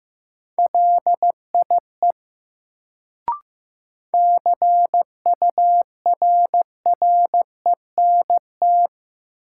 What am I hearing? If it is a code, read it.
Morse code, heard as a keyed tone, about 15 wpm, 700 Hz. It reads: LIE CURRENT
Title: Sentences from Top 1600 Words - New Word Drill Alt - 15wpm